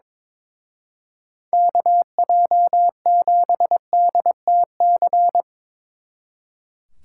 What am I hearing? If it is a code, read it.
Morse code, heard as a keyed tone, about 23 wpm, 700 Hz. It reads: KJ7DTC
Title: Speed Racing - USA 2x3 Call Signs 22wpm